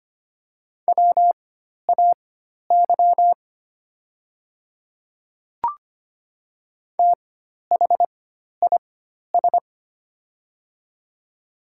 Morse code, heard as a keyed tone, about 25 wpm, spaced as for 12 wpm, 700 Hz. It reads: WAY THIS
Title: 4x Character Spacing - Top 100 Words - 25wpm